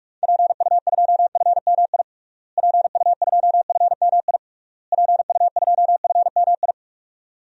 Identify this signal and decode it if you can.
Morse code, heard as a keyed tone, about 47 wpm, 700 Hz. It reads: PU2FMI PU2FMI PU2FMI
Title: DXCC-100 Call Signs - 6 Characters - Repeated 3x - 45wpm